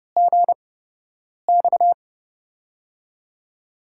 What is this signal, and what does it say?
Morse code, heard as a keyed tone, about 30 wpm, spaced as for 8 wpm, 700 Hz. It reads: GX
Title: USA Call Signs - 6 Characters - 8x Character Spacing - 30wpm